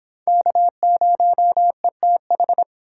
Morse code, heard as a keyed tone, about 26 wpm, 700 Hz. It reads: K0ETH